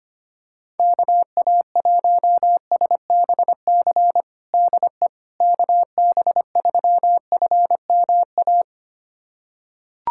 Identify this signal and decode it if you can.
Morse code, heard as a keyed tone, about 25 wpm, 700 Hz. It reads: KA1SBC DE KB3FMA
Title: USA Callsign DE Callsign - 6 Characters - 25wpm